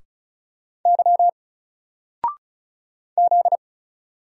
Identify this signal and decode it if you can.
Morse code, heard as a keyed tone, about 36 wpm, 700 Hz. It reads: Y Z
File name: Single-Letter-Number-35wpm